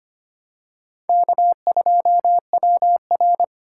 Morse code, heard as a keyed tone, about 25 wpm, 700 Hz. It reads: K2WR
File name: practice / 25wpm / Four Letter-Number Combinations - 25wpm